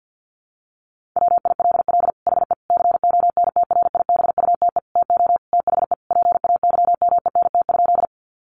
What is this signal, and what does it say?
Morse code, heard as a keyed tone, about 50 wpm, 700 Hz. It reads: WILL HE CONTRIBUTE TO THE PAYMENT?
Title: Sentences from Top 1400 Words 50wpm